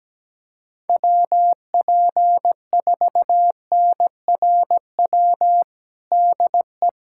Morse code, heard as a keyed tone, about 17 wpm, 700 Hz. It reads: WP4NRW DE
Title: USA Callsign DE Callsign - 6 Characters - 17wpm